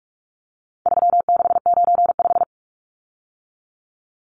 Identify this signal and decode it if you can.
Morse code, heard as a keyed tone, about 44 wpm, 700 Hz. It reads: 3695
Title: Numbers - 4-Digit 45wpm